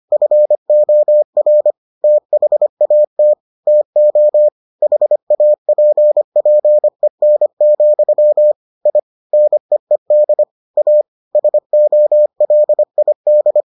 Morse code, heard as a keyed tone, about 25 wpm, 600 Hz. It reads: FOR THAT TO HAPPEN, I NEED A SOLID